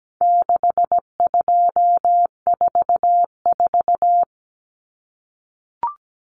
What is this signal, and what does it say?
Morse code, heard as a keyed tone, about 17 wpm, 700 Hz. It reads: 6244
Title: Numbers - 4-Digit 17wpm